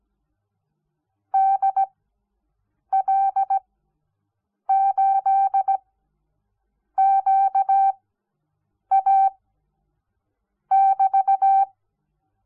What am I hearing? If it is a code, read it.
Morse code, heard as a keyed tone, about 17 wpm, spaced as for 6 wpm, 780 Hz. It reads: DL8QA<BT>
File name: Onlinelehrgang / morsen / DJ4UF Morsen06